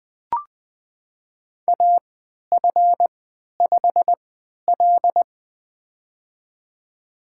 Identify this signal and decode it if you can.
Morse code, heard as a keyed tone, about 20 wpm, spaced as for 11 wpm, 700 Hz. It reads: AF5L
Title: USA Call Signs - 4 Characters - 3x Character Spacing - 20wpm